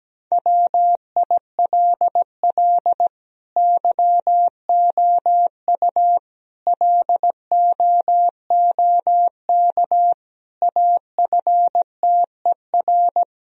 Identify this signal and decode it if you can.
Morse code, heard as a keyed tone, about 17 wpm, 700 Hz. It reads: WILL YOU LOOK AFTER
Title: Sentences from Top 1000 Words - Review Part 1 - 17wpm